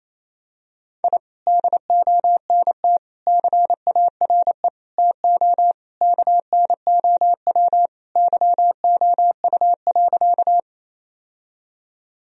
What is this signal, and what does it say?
Morse code, heard as a keyed tone, about 28 wpm, 700 Hz. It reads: I DONT CARE TO KNOW YOU.
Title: Sentences from Top 1000 Words - Review Part 1 - 28wpm